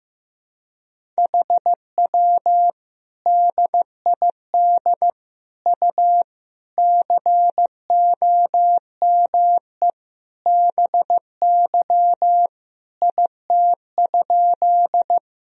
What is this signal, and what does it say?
Morse code, heard as a keyed tone, about 15 wpm, 700 Hz. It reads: HW DID U COME BY IT?
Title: Sentences from Top 100 Words and Abbreviations 15wpm